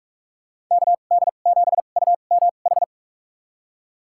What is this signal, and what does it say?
Morse code, heard as a keyed tone, about 46 wpm, spaced as for 32 wpm, 700 Hz. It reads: KD7UMH
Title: USA Call Signs - 6 Characters - 2x Character Spacing - 45wpm